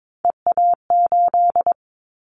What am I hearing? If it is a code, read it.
Morse code, heard as a keyed tone, about 22 wpm, 700 Hz. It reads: EA8
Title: Callsign Prefixes for Top 100 DXCC - 22wpm